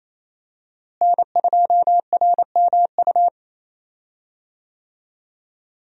Morse code, heard as a keyed tone, about 28 wpm, 700 Hz. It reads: N2RMU
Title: USA Call Signs - 5 Characters - Repeated 2x - 28wpm